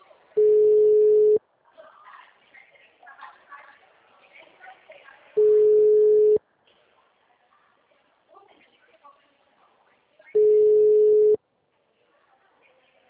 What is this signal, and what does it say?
Ringback tone (Europe, 425 Hz), with speech.
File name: out- -2011- - - .29780